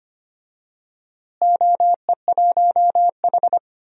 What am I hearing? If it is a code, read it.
Morse code, heard as a keyed tone, about 25 wpm, 700 Hz. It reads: OE1H
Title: DXCC-100 Call Signs - 4 Characters - Repeated 3x - 25wpm